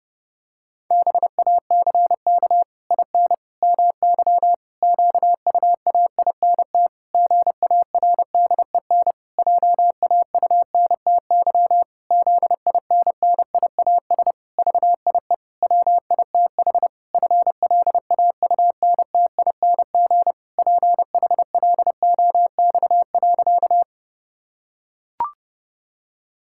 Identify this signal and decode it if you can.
Morse code, heard as a keyed tone, about 32 wpm, 700 Hz. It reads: BACK IN MY QUAINT GARDEN JAUNTY ZINNIAS VIE WITH FLAUNTING PHLOX.
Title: Pangram Challenge v2 - 30wpm